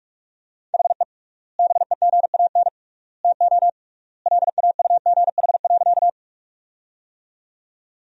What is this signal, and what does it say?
Morse code, heard as a keyed tone, about 46 wpm, 700 Hz. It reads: HE BEGAN TO LAUGH.
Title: 3x Word-Spacing - Sentences from Top 500 Words - 45wpm